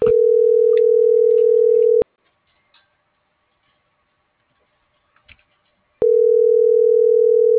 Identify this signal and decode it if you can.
Ringback tone (North America), 440 + 480 Hz.